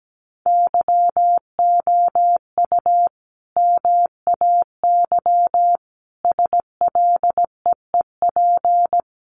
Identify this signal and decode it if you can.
Morse code, heard as a keyed tone, about 17 wpm, 700 Hz. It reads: YOU MAY SLEEP
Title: Sentences from Top 1500 Words 17wpm